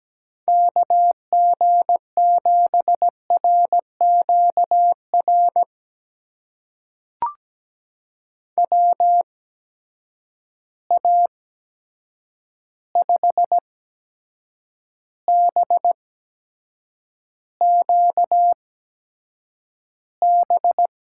Morse code, heard as a keyed tone, about 17 wpm, spaced as for 7 wpm, 700 Hz. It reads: KG7RQR WA5BQB
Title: USA Call Signs - 6 Characters - 8x Character Spacing - 17wpm